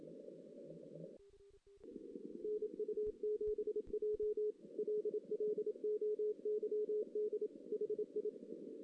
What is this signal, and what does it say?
Morse code, heard as a keyed tone, about 27 wpm, 420 Hz. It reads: NU7J LLOYD HI
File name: 2016 JAN CW NAQP